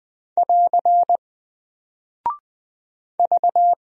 Morse code, heard as a keyed tone, about 20 wpm, 700 Hz. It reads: <AR> V
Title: Single Character (Letter:Number:Punctuation:Prosign) Alt 20wpm